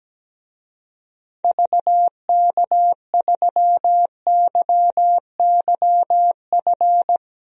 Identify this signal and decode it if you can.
Morse code, heard as a keyed tone, about 17 wpm, 700 Hz. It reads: VK3YYF
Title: DXCC-100 Call Signs - 6 Characters - 17wpm